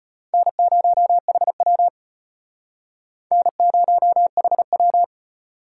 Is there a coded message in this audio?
Yes: Morse code from 0.3 to 5.0 s, heard as a keyed tone, about 35 wpm, 700 Hz. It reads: N0HW N0HW